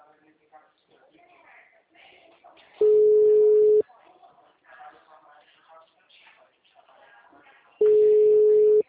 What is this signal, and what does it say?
Ringback tone (Europe, 425 Hz), with speech.
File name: out- -2017- - - .141